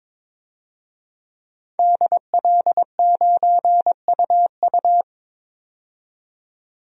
Morse code, heard as a keyed tone, about 22 wpm, 700 Hz. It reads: DL9UU
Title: DXCC-100 Call Signs - 5 Characters - 22wpm